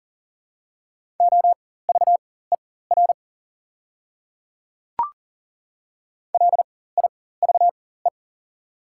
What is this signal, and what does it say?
Morse code, heard as a keyed tone, about 40 wpm, spaced as for 19 wpm, 700 Hz. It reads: OVER LIVE
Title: 4x Character Spacing - Top 200 Words - 40wpm